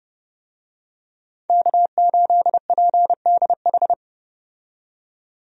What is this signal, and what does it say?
Morse code, heard as a keyed tone, about 30 wpm, 700 Hz. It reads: K8PDH
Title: USA Call Signs - 5 Characters - 30wpm